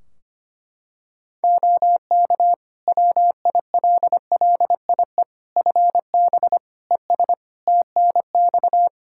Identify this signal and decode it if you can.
Morse code, heard as a keyed tone, about 25 wpm, 700 Hz. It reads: OK WILLIE FB ES TNX
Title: Practice QSO - Exchange 2 - Info - 25wpm